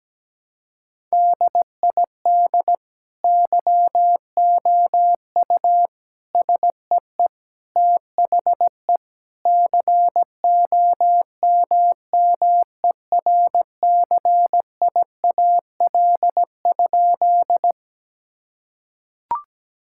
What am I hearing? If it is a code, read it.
Morse code, heard as a keyed tone, about 17 wpm, 700 Hz. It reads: DID YOU SEE THE COMMERCIAL?
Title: Sentences from Top 1000 Words - Review Part 5 - 17wpm